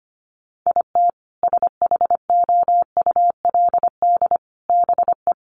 Morse code, heard as a keyed tone, about 25 wpm, 700 Hz. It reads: IT SHOULD BE
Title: Sets of 5 Words 25wpm